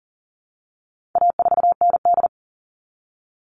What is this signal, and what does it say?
Morse code, heard as a keyed tone, about 40 wpm, 700 Hz. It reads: A4ND
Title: All Cut Numbers - 4-Digit 40wpm